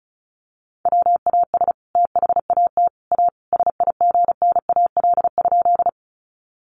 Morse code, heard as a keyed tone, about 34 wpm, 700 Hz. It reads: WAS THAT A SIGNAL?
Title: Sentences from Top 1400 Words 35wpm